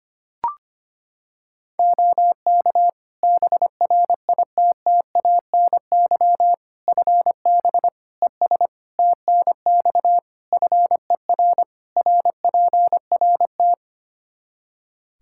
Morse code, heard as a keyed tone, about 25 wpm, 700 Hz. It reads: OK BRITTANY FB ES TNX FER RPRT
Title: Practice QSO - Exchange 2 - Info - 25wpm